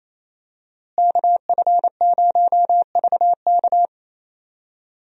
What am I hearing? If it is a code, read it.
Morse code, heard as a keyed tone, about 28 wpm, 700 Hz. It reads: KF0VK
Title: USA Call Signs - 5 Characters - Repeated 3x - 28wpm